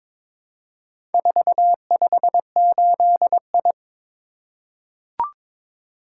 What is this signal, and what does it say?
Morse code, heard as a keyed tone, about 22 wpm, 700 Hz. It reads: 458I
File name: practice / 22wpm / Four Letter-Number Combinations - 22wpm